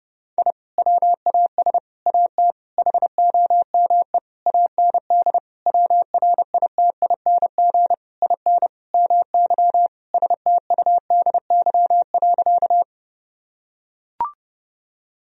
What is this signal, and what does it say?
Morse code, heard as a keyed tone, about 30 wpm, 700 Hz. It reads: I WAS AT HOME AND WRITING IN MY STUDY.
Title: Sentences from Top 1000 Words - Review Part 1 - 30wpm